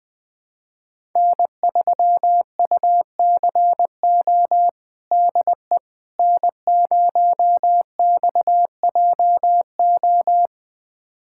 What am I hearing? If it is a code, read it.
Morse code, heard as a keyed tone, about 21 wpm, 700 Hz. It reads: N3UCO DE N0XJO
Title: USA Callsign DE Callsign - 5 Characters - 20wpm